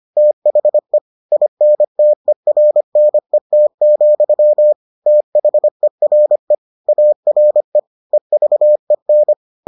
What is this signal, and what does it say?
Morse code, heard as a keyed tone, about 25 wpm, 600 Hz. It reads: THE INTERNET, THERE ARE EVEN